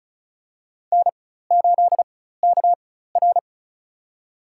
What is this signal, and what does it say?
Morse code, heard as a keyed tone, about 36 wpm, spaced as for 16 wpm, 700 Hz. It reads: N8KR
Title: USA Call Signs - 4 Characters - 4x Character Spacing - 35wpm